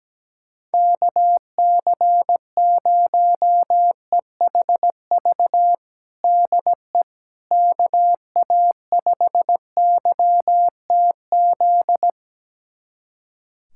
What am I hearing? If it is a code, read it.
Morse code, heard as a keyed tone, about 17 wpm, 700 Hz. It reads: KC0EHV DE KA5YTZ
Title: USA Callsign DE Callsign - 6 Characters - 17wpm